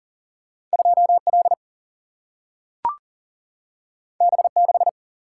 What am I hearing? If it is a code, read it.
Morse code, heard as a keyed tone, about 40 wpm, 700 Hz. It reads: 2 L B 6
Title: Single-Letter Single-Number 40wpm